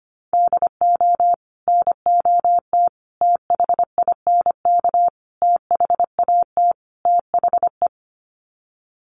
Morse code, heard as a keyed tone, about 25 wpm, 700 Hz. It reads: DO NOT THINK THAT THE
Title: Sets of 5 Words Encore2 25wpm